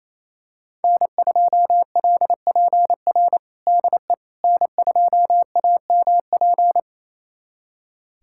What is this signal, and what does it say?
Morse code, heard as a keyed tone, about 28 wpm, 700 Hz. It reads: N2LPR DE N2AMP
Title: USA Callsign DE Callsign - 5 Characters - 28wpm